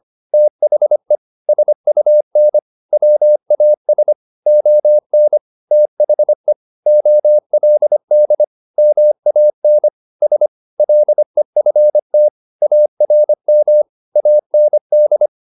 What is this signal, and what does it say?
Morse code, heard as a keyed tone, about 25 wpm, 600 Hz. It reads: THE SUN WAS ON THE OLD MAN S LEFT ARM AND
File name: the old man and the sea part 1 25wpm